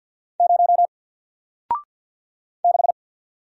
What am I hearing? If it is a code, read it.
Morse code, heard as a keyed tone, about 48 wpm, 700 Hz. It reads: 0 6
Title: Single Letter-Number Alt 50wpm